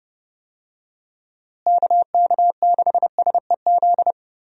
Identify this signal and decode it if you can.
Morse code, heard as a keyed tone, about 30 wpm, 700 Hz. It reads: KK6SEZ
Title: USA Call Signs - 6 Characters - Repeated 2x - 30wpm